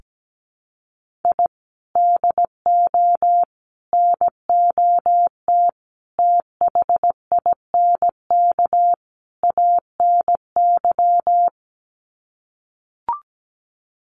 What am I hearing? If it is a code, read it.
Morse code, heard as a keyed tone, about 17 wpm, 700 Hz. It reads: I DO NOT THINK ANY
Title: Sets of 5 Words Encore 17wpm